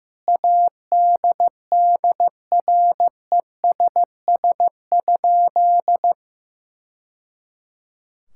Morse code, heard as a keyed tone, about 15 wpm, 700 Hz. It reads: ADDRESS?
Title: Sentences from Top 1000 Words - Review Part 6 - 15wpm